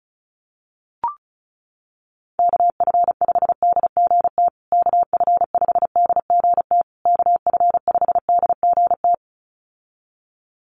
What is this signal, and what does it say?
Morse code, heard as a keyed tone, about 34 wpm, 700 Hz. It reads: KF5DGT KF5DGT KF5DGT
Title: USA Call Signs Repeated 3x 35wpm